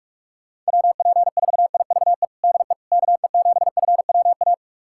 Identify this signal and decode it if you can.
Morse code, heard as a keyed tone, about 45 wpm, 700 Hz. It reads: WP4IVE DE KE7FWA
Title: USA Callsign DE Callsign 45wpm